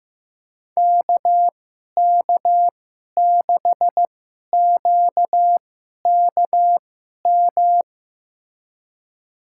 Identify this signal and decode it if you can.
Morse code, heard as a keyed tone, about 16 wpm, spaced as for 11 wpm, 700 Hz. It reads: KK6QKM
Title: USA Call Signs - 6 Characters - 2x Character Spacing - 15wpm